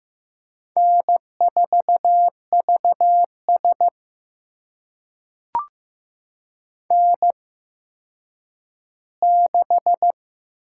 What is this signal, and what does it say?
Morse code, heard as a keyed tone, about 15 wpm, spaced as for 9 wpm, 700 Hz. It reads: N4VS N6
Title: USA Call Signs - 4 Characters - 8x Character Spacing - 15wpm